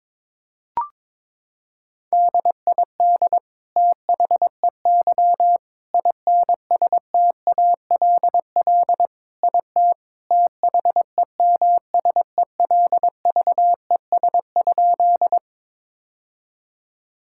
Morse code, heard as a keyed tone, about 21 wpm, 700 Hz. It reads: DID THEY INSTALL IT THEMSELVES?
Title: Sentences from Top 1600 Words 22wpm